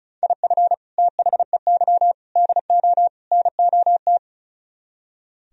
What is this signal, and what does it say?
Morse code, heard as a keyed tone, about 44 wpm, 700 Hz. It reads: IF THEY DO NOT
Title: Speed-Racing - Sets of 4 Words 35wpm